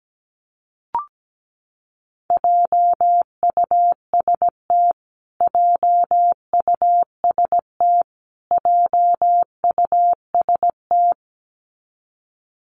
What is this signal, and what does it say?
Morse code, heard as a keyed tone, about 17 wpm, 700 Hz. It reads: JUST JUST JUST
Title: Top 300 Words Repeated 17wpm